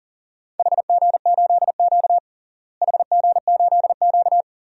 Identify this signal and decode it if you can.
Morse code, heard as a keyed tone, about 40 wpm, 700 Hz. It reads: HG8Q HG8Q
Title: DXCC-100 Call Signs - 4 Characters - Repeated 3x - 40wpm